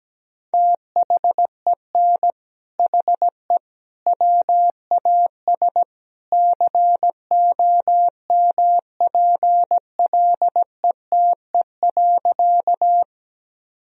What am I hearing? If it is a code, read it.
Morse code, heard as a keyed tone, about 17 wpm, 700 Hz. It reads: THEN HE WAS COMPLETE.
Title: Sentences from Top 1000 Words - Review Part 2 - 17wpm